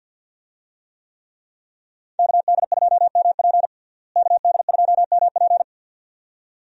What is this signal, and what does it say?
Morse code, heard as a keyed tone, about 51 wpm, 700 Hz. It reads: KD2MP KD2MP
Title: USA Call Signs - 5 Characters - Repeated 2x - 50wpm